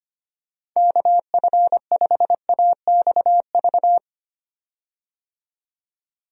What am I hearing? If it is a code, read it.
Morse code, heard as a keyed tone, about 25 wpm, 700 Hz. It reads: KF5AXV
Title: USA Call Signs - 6 Characters - 25wpm